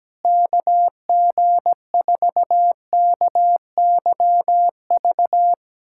Morse code, heard as a keyed tone, about 17 wpm, 700 Hz. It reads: KG4KYV
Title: USA Callsign DE Callsign - 6 Characters - 17wpm